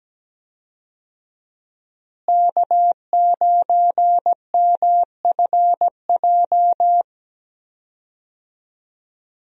Morse code, heard as a keyed tone, about 17 wpm, 700 Hz. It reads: K9MFJ
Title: USA Call Signs - 5 Characters - Repeated 3x - 17wpm